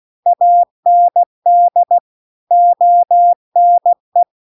Morse code, heard as a keyed tone, about 16 wpm, 700 Hz. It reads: AND ONE
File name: CW-The-Big-Twoheaded-River-2-16WPM